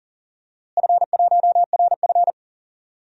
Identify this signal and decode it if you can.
Morse code, heard as a keyed tone, about 38 wpm, 700 Hz. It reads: F1RF
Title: DXCC-100 Call Signs - 4 Characters - Repeated 3x - 40wpm